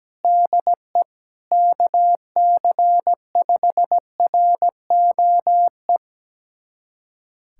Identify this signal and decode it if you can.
Morse code, heard as a keyed tone, about 17 wpm, 700 Hz. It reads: DE KC5ROE
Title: USA Callsign DE Callsign - 6 Characters - 17wpm